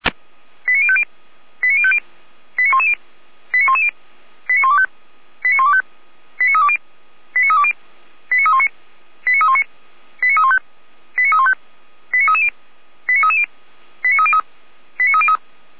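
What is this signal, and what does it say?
Five-tone selective call (ZVEI), heard as quick runs of five tones, tones of about 70 ms. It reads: 89050 89050 89110 89110 89125 89125 89320 89320 89319 89319 89315 89315 89330 89330 89303 89303
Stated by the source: Received on an SDR as a ZVEI five-tone call, address 89042.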